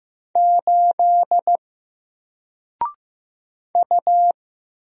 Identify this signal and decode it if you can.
Morse code, heard as a keyed tone, about 15 wpm, 700 Hz. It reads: 8 U
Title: Single Letter-Number 15wpm